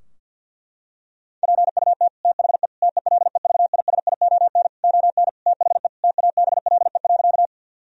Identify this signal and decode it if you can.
Morse code, heard as a keyed tone, about 50 wpm, 700 Hz. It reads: PUT THE TELEVISION ON THE TABLE.
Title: Sentences from Top 1000 Words - Review Part 4 - 50wpm